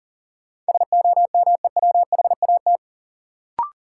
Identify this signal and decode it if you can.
Morse code, heard as a keyed tone, about 40 wpm, 700 Hz. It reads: SOMEWHAT
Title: Sentences from Top 1300 Words - New Word Drill Alt - 40wpm